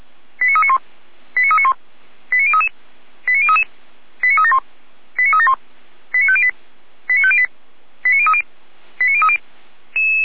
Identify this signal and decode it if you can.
Five-tone selective call (ZVEI), heard as quick runs of five tones, tones of about 80 ms. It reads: 89391 89391 89033 89033 89371 89371 89508 89508 89030 89030
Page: ZVEI